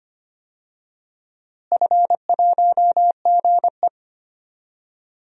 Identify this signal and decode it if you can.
Morse code, heard as a keyed tone, about 25 wpm, 700 Hz. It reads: F1GE
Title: DXCC-100 Call Signs - 4 Characters - Repeated 3x - 25wpm